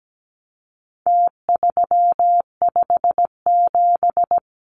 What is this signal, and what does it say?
Morse code, heard as a keyed tone, about 17 wpm, 700 Hz. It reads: T357
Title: Numbers - Contest Style - 4-Digit 17wpm